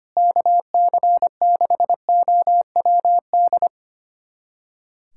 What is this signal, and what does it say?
Morse code, heard as a keyed tone, about 25 wpm, 700 Hz. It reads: KC6OWD
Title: USA Callsign DE Callsign - 6 Characters - 25wpm